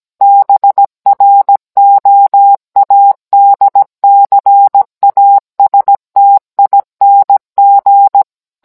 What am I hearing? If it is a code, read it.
Morse code, heard as a keyed tone, about 17 wpm, 800 Hz. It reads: BROADCASTING